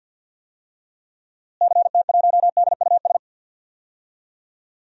Morse code, heard as a keyed tone, about 52 wpm, 700 Hz. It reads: CT1DUS
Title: DXCC-100 Call Signs - 6 Characters - Repeated 2x - 50wpm